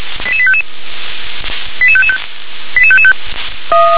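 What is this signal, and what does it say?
Five-tone selective call (ZVEI), heard as quick runs of five tones, tones of about 80 ms. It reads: 88855 88555 88555
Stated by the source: ZVEI five-tone selective call received by SDR.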